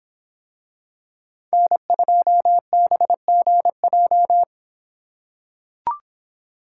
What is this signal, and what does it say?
Morse code, heard as a keyed tone, about 25 wpm, 700 Hz. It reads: N2BGJ